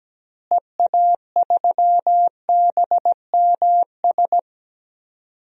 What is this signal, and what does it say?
Morse code, heard as a keyed tone, about 17 wpm, 700 Hz. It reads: EA3BMS
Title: DXCC-100 Call Signs - 6 Characters - Repeated 2x - 17wpm